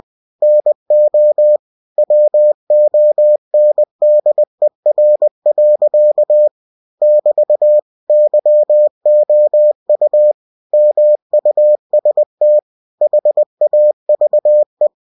Morse code, heard as a keyed tone, about 20 wpm, 600 Hz. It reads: NO WONDER. <BT> YOU MUST HAVE
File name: cw-books / winnie-the-pooh / winnie-20wpm-page-0027